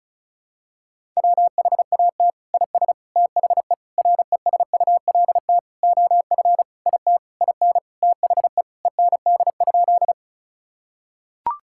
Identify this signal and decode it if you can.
Morse code, heard as a keyed tone, about 36 wpm, 700 Hz. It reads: WHAT IS THE RESULT OF IT IN THE END?
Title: Sentences from Top 1000 Words - Review Part 5 - 35wpm